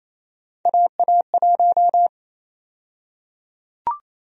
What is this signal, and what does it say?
Morse code, heard as a keyed tone, about 28 wpm, 700 Hz. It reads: AA1
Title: USA Call Sign Prefixes - 28wpm